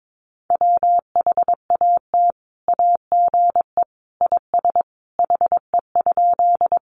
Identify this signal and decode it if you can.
Morse code, heard as a keyed tone, about 22 wpm, 700 Hz. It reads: WHAT AGE IS HE?
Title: Sentences from Top 500 Words 22wpm